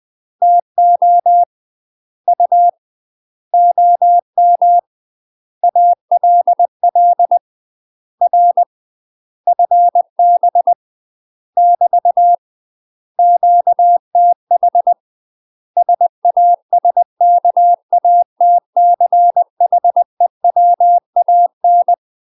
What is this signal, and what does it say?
Morse code, heard as a keyed tone, about 20 wpm, 700 Hz. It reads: TO U OM ALL R FB <BT> QTH SASKATCHEWAN
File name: UK MASTER QSOs TRAINING / 20WPM / UK MASTER QSOs 0050